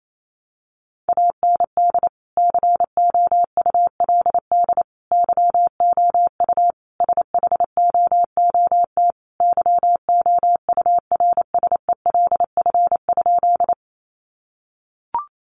Morse code, heard as a keyed tone, about 28 wpm, 700 Hz. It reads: AND COULD YOU SHOOT YOURSELF?
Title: Sentences from Top 1200 Words 28wpm